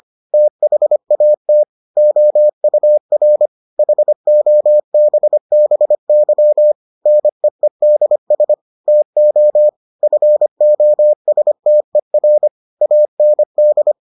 Morse code, heard as a keyed tone, about 25 wpm, 600 Hz. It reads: THAT OUR HOBBY NEEDS TO FOSTER AND